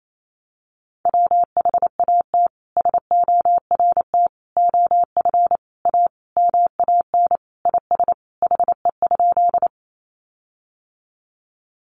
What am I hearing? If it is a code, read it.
Morse code, heard as a keyed tone, about 28 wpm, 700 Hz. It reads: WHAT SORT OF A MAN IS HE?
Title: Sentences from Top 1300 Words 28wpm